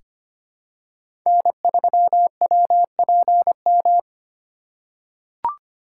Morse code, heard as a keyed tone, about 25 wpm, 700 Hz. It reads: N3WPM
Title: USA Call Signs - 5 Characters - 4x Character Spacing - 25wpm